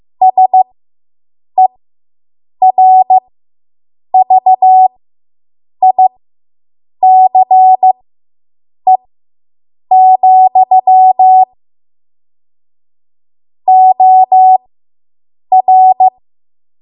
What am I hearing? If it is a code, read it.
Morse code, heard as a keyed tone, about 15 wpm, spaced as for 7 wpm, 750 Hz. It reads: SERVICE, OR